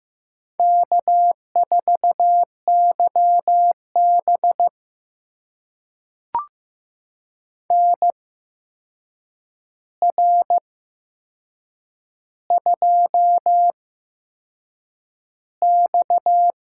Morse code, heard as a keyed tone, about 15 wpm, spaced as for 6 wpm, 700 Hz. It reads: K4YB NR2X
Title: USA Call Signs - 4 Characters - 8x Character Spacing - 15wpm